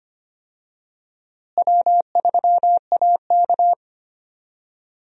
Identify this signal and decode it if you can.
Morse code, heard as a keyed tone, about 25 wpm, 700 Hz. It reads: W3AK
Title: USA Call Signs - 4 Characters - Repeated 2x - 25wpm